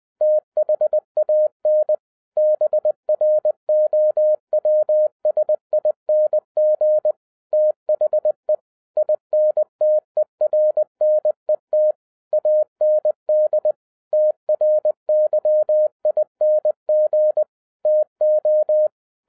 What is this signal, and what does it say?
Morse code, heard as a keyed tone, about 20 wpm, 610 Hz. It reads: THAN BROWSING THE INTERNET AND TRYING TO